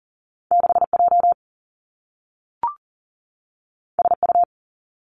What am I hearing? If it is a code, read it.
Morse code, heard as a keyed tone, about 39 wpm, 700 Hz. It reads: 6J SU
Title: Two Challenging Letter-Number Combinations 40wpm